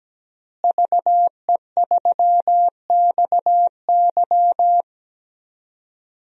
Morse code, heard as a keyed tone, about 17 wpm, 700 Hz. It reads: VE3XY
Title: DXCC-100 Call Signs - 5 Characters - Repeated 3x - 17wpm